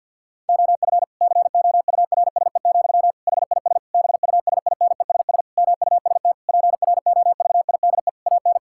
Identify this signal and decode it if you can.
Morse code, heard as a keyed tone, about 46 wpm, 700 Hz. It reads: OF COURSE, HIS BUSINESS MUST PROVIDE AN
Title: Sentences from Top 1000 Words - Review Part 5 - 50wpm